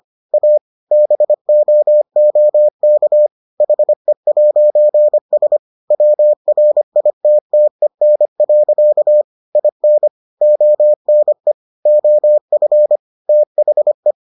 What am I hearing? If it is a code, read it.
Morse code, heard as a keyed tone, about 25 wpm, 600 Hz. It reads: A BOOK HE'S WRITTEN. IN ONE OF THE